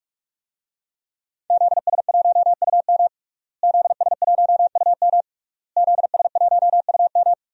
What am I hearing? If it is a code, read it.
Morse code, heard as a keyed tone, about 46 wpm, 700 Hz. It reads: ZS1UM ZS1UM ZS1UM
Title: DXCC-100 Call Signs - 5 Characters - Repeated 3x - 45wpm